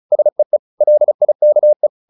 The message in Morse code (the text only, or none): SEE LIKE